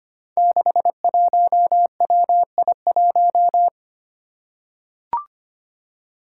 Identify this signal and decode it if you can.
Morse code, heard as a keyed tone, about 25 wpm, 700 Hz. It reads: 61WI1
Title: Five Challenging Letter-Number Combinations 25wpm